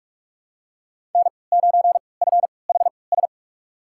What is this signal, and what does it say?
Morse code, heard as a keyed tone, about 46 wpm, spaced as for 25 wpm, 700 Hz. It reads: N9FHS
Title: USA Call Signs - 5 Characters - 3x Character Spacing - 45wpm